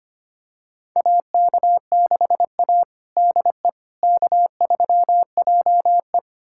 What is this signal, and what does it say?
Morse code, heard as a keyed tone, about 25 wpm, 700 Hz. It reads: AK6A DE K3JE
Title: USA Callsign DE Callsign - 4 Characters - 25wpm